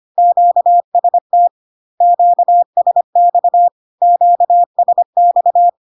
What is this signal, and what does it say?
Morse code, heard as a keyed tone, about 25 wpm, 700 Hz. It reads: QST QSX QSX
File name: Q-codes most common repeated 3 times 25 25wpm